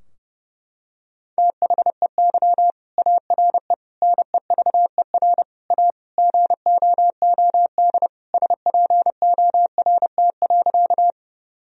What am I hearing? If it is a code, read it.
Morse code, heard as a keyed tone, about 30 wpm, 700 Hz. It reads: THEY ARE NEVER A GOOD SPORT.
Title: Sentences from Top 1000 Words - Review Part 4 - 30wpm